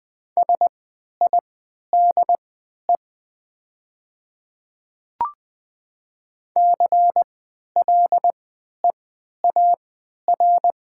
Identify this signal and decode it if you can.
Morse code, heard as a keyed tone, about 20 wpm, spaced as for 11 wpm, 700 Hz. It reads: SIDE CLEAR
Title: 3x Character Spacing - Top 500 Words - 20wpm